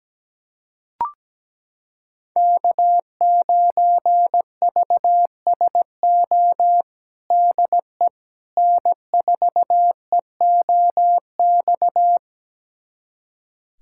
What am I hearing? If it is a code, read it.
Morse code, heard as a keyed tone, about 17 wpm, 700 Hz. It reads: K9VSO DE N4EOX
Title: USA Callsign DE Callsign - 5 Characters - 17wpm